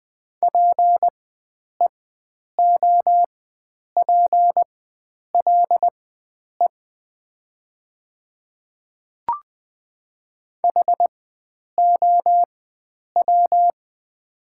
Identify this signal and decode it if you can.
Morse code, heard as a keyed tone, about 20 wpm, spaced as for 9 wpm, 700 Hz. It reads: PEOPLE HOW
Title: 4x Character Spacing - Top 100 Words - 20wpm